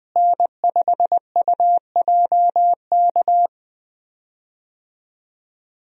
Morse code, heard as a keyed tone, about 20 wpm, 700 Hz. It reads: N5UJK
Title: USA Call Signs - 5 Characters - Repeated 3x - 20wpm